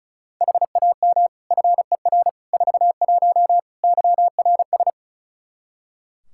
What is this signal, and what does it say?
Morse code, heard as a keyed tone, about 34 wpm, 700 Hz. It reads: HAM FER 41 YRS
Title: Practice QSO - Exchange 3 - More Info - 35wpm